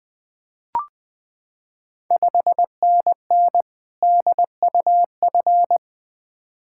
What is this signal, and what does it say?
Morse code, heard as a keyed tone, about 20 wpm, 690 Hz. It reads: 5NN DUF
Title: Ontario QSO Party Exchange 20wpm